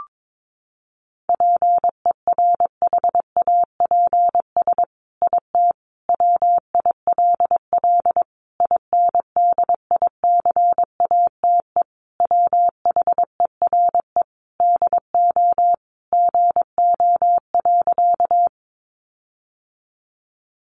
Morse code, heard as a keyed tone, about 22 wpm, 700 Hz. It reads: PERHAPS IT WILL INDICATE WHERE DO GO.
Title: Sentences from Top 1300 Words 22wpm